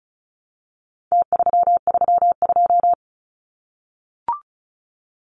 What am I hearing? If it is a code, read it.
Morse code, heard as a keyed tone, about 34 wpm, 700 Hz. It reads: T332
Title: Numbers - Contest Style - 4-Digit 35wpm